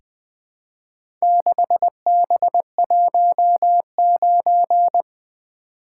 Morse code, heard as a keyed tone, about 20 wpm, 700 Hz. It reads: 6B19
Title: Four Letter-Number Combinations - 20wpm